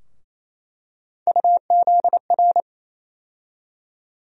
Morse code, heard as a keyed tone, about 28 wpm, 700 Hz. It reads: UZR
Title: USA Call Sign Postfixes - 3 Letters 28wpm